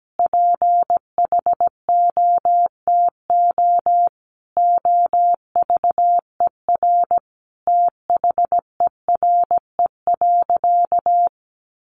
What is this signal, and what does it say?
Morse code, heard as a keyed tone, about 17 wpm, 700 Hz. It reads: PHOTO OVER THERE.
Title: Sentences from Top 1200 Words 17wpm